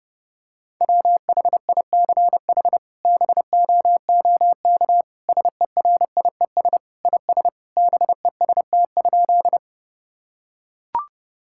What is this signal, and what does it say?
Morse code, heard as a keyed tone, about 30 wpm, 700 Hz. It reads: WHICH BOOK SERIES IS BEST?
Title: Sentences from Top 1000 Words - Review Part 5 - 30wpm